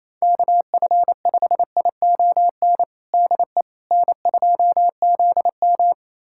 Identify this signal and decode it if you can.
Morse code, heard as a keyed tone, about 27 wpm, 700 Hz. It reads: KF5ION DE N2ZM
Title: USA Callsign DE Callsign 28wpm